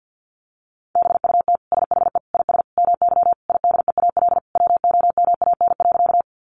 Morse code, heard as a keyed tone, about 50 wpm, 700 Hz. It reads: BUT SHE IS MY IDEAL WOMAN.
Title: Sentences from Top 1400 Words 50wpm